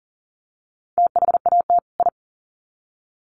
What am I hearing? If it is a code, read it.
Morse code, heard as a keyed tone, about 40 wpm, 700 Hz. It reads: THAT I
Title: Sets of 2 Words 40wpm